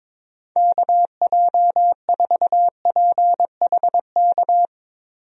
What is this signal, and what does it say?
Morse code, heard as a keyed tone, about 22 wpm, 700 Hz. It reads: KJ4PHK
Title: USA Call Signs - 6 Characters - Repeated 3x - 22wpm